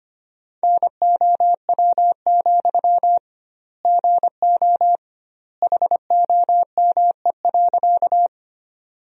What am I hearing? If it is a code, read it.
Morse code, heard as a keyed tone, about 25 wpm, 700 Hz. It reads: NOW, GO HOME.
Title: 2x Word-Spacing - Sentences from Top 200 Words - 25wpm